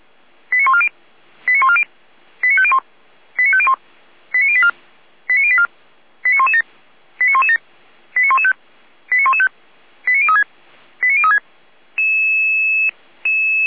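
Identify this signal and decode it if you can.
Five-tone selective call (ZVEI), heard as quick runs of five tones, tones of about 70 ms. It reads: 89140 89140 89501 89501 89084 89084 89117 89117 89115 89115 89036 89036
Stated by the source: Five-tone ZVEI selective call received by SDR, from the Feuerwehr Orlamünde fire brigade audio server.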